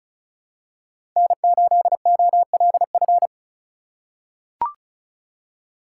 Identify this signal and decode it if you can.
Morse code, heard as a keyed tone, about 36 wpm, 700 Hz. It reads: N8OLF
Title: USA Call Signs - 5 Characters - 2x Character Spacing - 35wpm